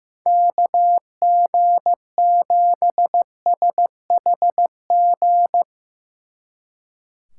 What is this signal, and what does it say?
Morse code, heard as a keyed tone, about 15 wpm, 700 Hz. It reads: KG7SHG
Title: Speed Racing - USA 2x3 Call Signs 15wpm